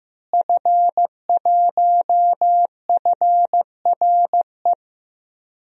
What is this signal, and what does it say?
Morse code, heard as a keyed tone, about 15 wpm, 700 Hz. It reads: F1FRE
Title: DXCC-100 Call Signs - 5 Characters - Repeated 3x - 15wpm